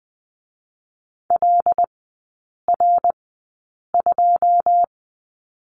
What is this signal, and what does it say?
Morse code, heard as a keyed tone, about 20 wpm, spaced as for 8 wpm, 700 Hz. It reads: LR2
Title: 087 Lesson 15e - Number 5 - Three Characters 20wpm